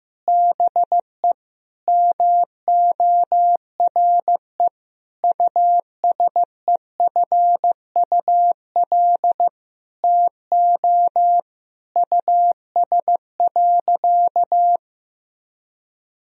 Morse code, heard as a keyed tone, about 15 wpm, 700 Hz. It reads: BE MORE USEFUL TO US.
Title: Sentences from Top 1000 Words - Review Part 4 - 15wpm